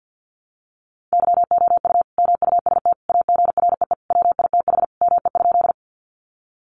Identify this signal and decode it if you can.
Morse code, heard as a keyed tone, about 49 wpm, 700 Hz. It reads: YOU MUST AGREE WITH ME?
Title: Sentences from Top 1100 Words 50wpm